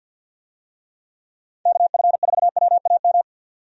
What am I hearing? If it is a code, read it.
Morse code, heard as a keyed tone, about 51 wpm, 700 Hz. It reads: KF4WAM
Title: USA Call Signs - 6 Characters - Repeated 2x - 50wpm